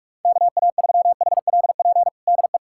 Morse code, heard as a keyed tone, about 46 wpm, 700 Hz. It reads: KA3HLP DE
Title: USA Callsign DE Callsign 45wpm